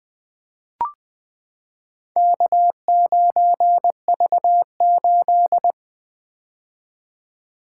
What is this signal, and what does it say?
Morse code, heard as a keyed tone, about 20 wpm, 700 Hz. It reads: K9V8
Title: Four Letter-Number Combinations - 20wpm